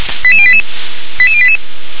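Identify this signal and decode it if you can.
Five-tone selective call (ZVEI), heard as quick runs of five tones, tones of about 70 ms. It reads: 88088 88088
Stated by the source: Five-tone ZVEI selective call, received by SDR.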